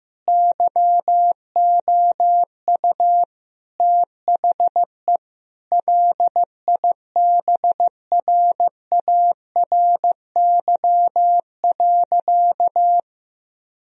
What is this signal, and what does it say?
Morse code, heard as a keyed tone, about 15 wpm, 700 Hz. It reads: YOU THE LIBRARY.
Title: Sentences from Top 1000 Words - Review Part 4 - 15wpm